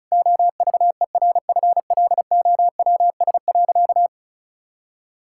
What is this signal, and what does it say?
Morse code, heard as a keyed tone, about 36 wpm, 700 Hz. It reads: OVERFLOWS.
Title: Pangram Challenge v2 - 35wpm